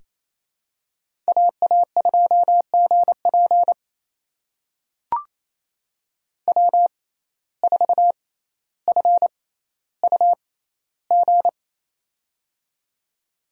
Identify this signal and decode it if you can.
Morse code, heard as a keyed tone, about 28 wpm, spaced as for 14 wpm, 700 Hz. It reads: AA2GP W4FUG
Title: USA Call Signs - 5 Characters - 6x Character Spacing - 28wpm